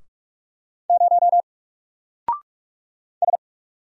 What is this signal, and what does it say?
Morse code, heard as a keyed tone, about 46 wpm, 700 Hz. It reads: 0 S
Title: Single Letter-Number 45wpm